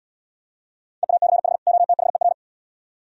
Morse code, heard as a keyed tone, about 72 wpm, 700 Hz. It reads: EACH OTHER